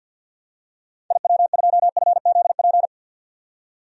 Morse code, heard as a keyed tone, about 48 wpm, 700 Hz. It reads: IW2FZP
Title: DXCC-100 Call Signs - 6 Characters - 50wpm